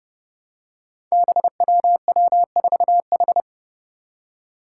Morse code, heard as a keyed tone, about 30 wpm, 700 Hz. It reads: BWW4H
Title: Five Challenging Letter-Number Combinations 30wpm